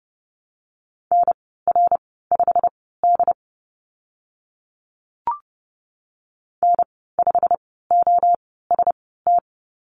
Morse code, heard as a keyed tone, about 30 wpm, spaced as for 17 wpm, 700 Hz. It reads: NR5D N5OST
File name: Lesson 15g - Number 5 - Call Signs 30wpm